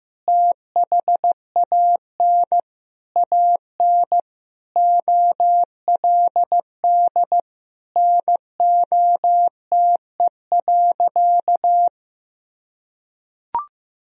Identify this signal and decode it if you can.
Morse code, heard as a keyed tone, about 16 wpm, 700 Hz. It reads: THAN AN OLD NOTE.
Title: Sentences from Top 1000 Words - Review Part 2 - 15wpm